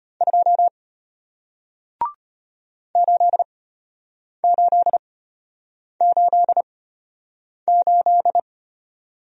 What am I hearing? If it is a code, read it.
Morse code, heard as a keyed tone, about 34 wpm, spaced as for 8 wpm, 700 Hz. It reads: 2 8888